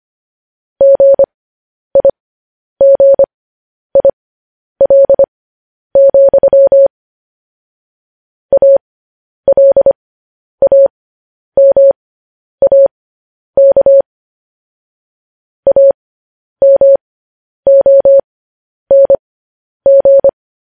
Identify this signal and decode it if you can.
Morse code, heard as a keyed tone, about 25 wpm, spaced as for 10 wpm, 550 Hz. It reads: GIGIL, ALAMAK AMONG
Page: news-headlines-in-morse-code-at-10-wpm